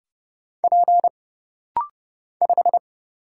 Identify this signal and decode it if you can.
Morse code, heard as a keyed tone, about 31 wpm, 700 Hz. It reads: P 5
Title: Single Letter-Number Alt - Rapid-Fire 30wpm